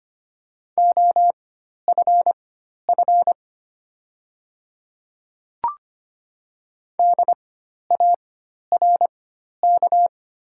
Morse code, heard as a keyed tone, about 25 wpm, spaced as for 12 wpm, 700 Hz. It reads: OFF DARK
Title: 4x Character Spacing - Top 500 Words - 25wpm